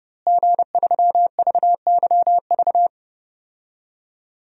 Morse code, heard as a keyed tone, about 30 wpm, 700 Hz. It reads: G3VYV
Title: DXCC-100 Call Signs - 5 Characters - Repeated 3x - 30wpm